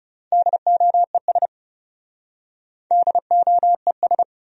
Morse code, heard as a keyed tone, about 40 wpm, 700 Hz. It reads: DOES DOES